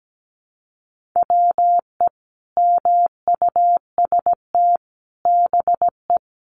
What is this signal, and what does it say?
Morse code, heard as a keyed tone, about 17 wpm, 700 Hz. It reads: WE MUST BE
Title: Sentences from Top 1400 Words 17wpm